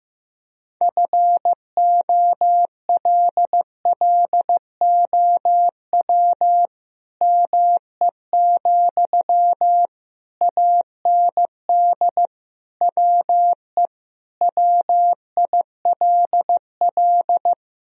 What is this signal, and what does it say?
Morse code, heard as a keyed tone, about 15 wpm, 700 Hz. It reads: FOLLOW ME, AND WE WILL
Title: Sentences from Top 1000 Words - Review Part 3 - 15wpm